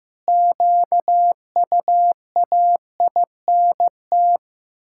Morse code, heard as a keyed tone, about 15 wpm, 700 Hz. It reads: QUAINT
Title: Pangram Challenge v2 - 15wpm